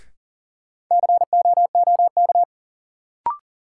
Morse code, heard as a keyed tone, about 40 wpm, 700 Hz. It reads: COOK
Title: Sentences from Top 700 Words - New Word Drill - 40wpm